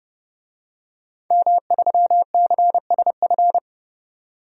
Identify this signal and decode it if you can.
Morse code, heard as a keyed tone, about 31 wpm, 700 Hz. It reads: M3CSF
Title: DXCC-100 Call Signs - 5 Characters - 30wpm